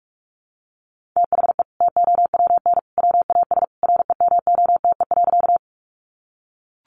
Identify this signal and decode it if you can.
Morse code, heard as a keyed tone, about 44 wpm, 700 Hz. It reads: THE TOWN WAS REMOTE.
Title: Sentences from Top 1300 Words 45wpm